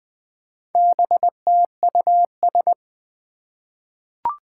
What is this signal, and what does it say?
Morse code, heard as a keyed tone, about 20 wpm, 700 Hz. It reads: BTUS
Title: Four Challenging Letter-Number Combinations 20wpm